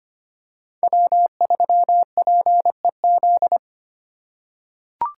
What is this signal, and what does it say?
Morse code, heard as a keyed tone, about 25 wpm, 700 Hz. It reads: W3PEZ
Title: USA Call Signs - 4x Character Spacing - 25wpm